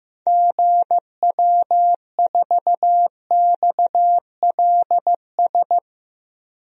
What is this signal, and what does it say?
Morse code, heard as a keyed tone, about 16 wpm, 700 Hz. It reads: GW4XLS
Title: DXCC-100 Call Signs - 6 Characters - Repeated 3x - 15wpm